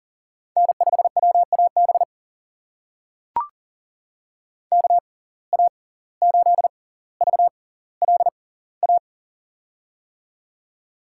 Morse code, heard as a keyed tone, about 40 wpm, spaced as for 19 wpm, 700 Hz. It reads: N5WAB KA8VLA
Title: USA Call Signs - 6x Character Spacing - 40wpm